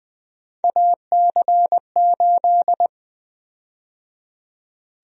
Morse code, heard as a keyed tone, about 20 wpm, 700 Hz. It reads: AC8
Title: USA Call Sign Prefixes - 20wpm